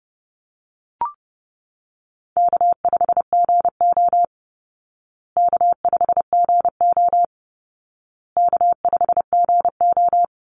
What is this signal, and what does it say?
Morse code, heard as a keyed tone, about 30 wpm, 700 Hz. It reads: K5GO K5GO K5GO
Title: Easy Call Signs Repeated 3x - 30wpm